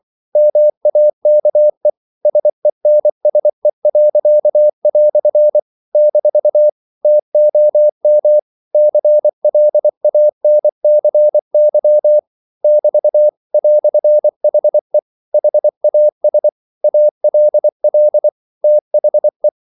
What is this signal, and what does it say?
Morse code, heard as a keyed tone, about 24 wpm, 600 Hz. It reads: MAKE SENSE." - TOM CLANCY <BT> "HE HAS ALL THE
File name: CW / Practice / EnglishQuotations0424